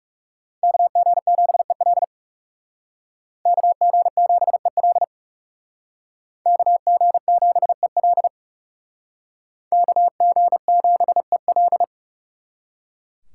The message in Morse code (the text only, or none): KG7EL KG7EL KG7EL KG7EL